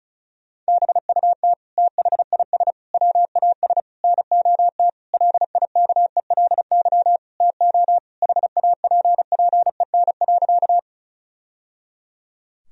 Morse code, heard as a keyed tone, about 36 wpm, 700 Hz. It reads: BUT THIS WAS NOT LIKELY TO HAPPEN.
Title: Sentences from Top 1000 Words - Review Part 4 - 35wpm